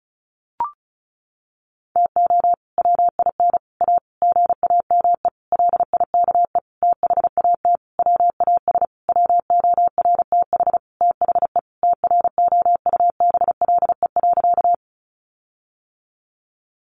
Morse code, heard as a keyed tone, about 34 wpm, 700 Hz. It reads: TO WIN A GAME LIKE THAT WAS WORTH THE TROUBLE.
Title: Sentences from Top 1500 Words 35wpm